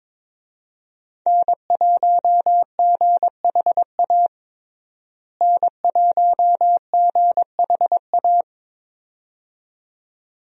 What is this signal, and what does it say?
Morse code, heard as a keyed tone, about 22 wpm, 700 Hz. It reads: N1GHA N1GHA
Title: USA Call Signs - 5 Characters - Repeated 2x - 22wpm